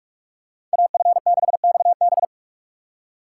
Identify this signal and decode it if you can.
Morse code, heard as a keyed tone, about 44 wpm, 700 Hz. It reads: AF6XB
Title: USA Call Signs - 6x Character Spacing - 45wpm